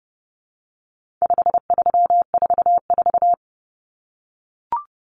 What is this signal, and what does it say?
Morse code, heard as a keyed tone, about 30 wpm, 700 Hz. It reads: 5344
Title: Numbers - 4-Digit 30wpm